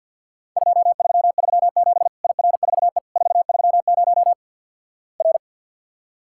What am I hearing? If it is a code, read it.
Morse code, heard as a keyed tone, about 50 wpm, 700 Hz. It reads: 2337 IR4E 430 R
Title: Worked All Europe DX Contest - QTC Exchange with Repeats - 50wpm